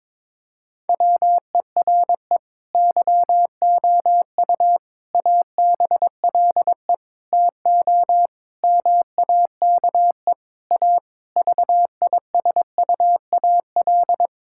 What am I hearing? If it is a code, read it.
Morse code, heard as a keyed tone, about 22 wpm, 700 Hz. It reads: WERE YOU ABLE TO MAKE A VISUAL
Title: Sentences from Top 1000 Words - Review Part 6 - 22wpm